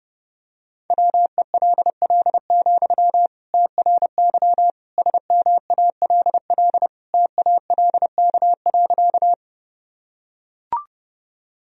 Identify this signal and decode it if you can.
Morse code, heard as a keyed tone, about 30 wpm, 700 Hz. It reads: WELL, TRY SMALL TALK.
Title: Sentences from Top 1000 Words - Review Part 1 - 30wpm